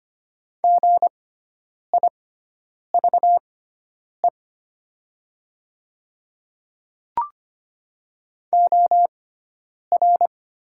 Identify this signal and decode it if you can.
Morse code, heard as a keyed tone, about 25 wpm, spaced as for 9 wpm, 700 Hz. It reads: GIVE OR
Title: 6x Character Spacing - Top 200 Words - 25wpm